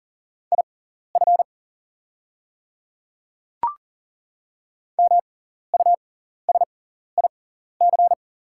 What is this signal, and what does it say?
Morse code, heard as a keyed tone, about 40 wpm, spaced as for 14 wpm, 690 Hz. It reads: IF MUSIC
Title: 6x Character Spacing - Top 300 Words - 40wpm